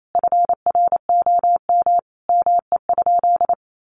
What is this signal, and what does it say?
Morse code, heard as a keyed tone, about 28 wpm, 700 Hz. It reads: FROM ME?
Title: Sentences from Top 1500 Words 28wpm